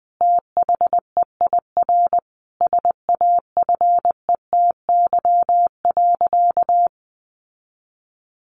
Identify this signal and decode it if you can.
Morse code, heard as a keyed tone, about 20 wpm, 700 Hz. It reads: THEIR SAFETY.
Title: Sentences from Top 700 Words 20wpm